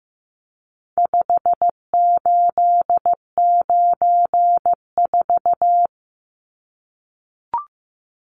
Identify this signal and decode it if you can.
Morse code, heard as a keyed tone, about 15 wpm, 700 Hz. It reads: 5894
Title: Numbers - 4-Digit 15wpm